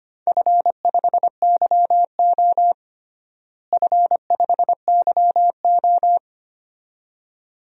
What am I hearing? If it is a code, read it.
Morse code, heard as a keyed tone, about 25 wpm, 700 Hz. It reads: F5YO F5YO
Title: DXCC-100 Call Signs - 4 Characters - Repeated 2x - 25wpm